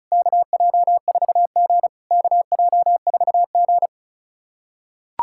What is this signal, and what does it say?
Morse code, heard as a keyed tone, about 34 wpm, 690 Hz. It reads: KJ4G KJ4G
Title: SOTA QSO Practice 35wpm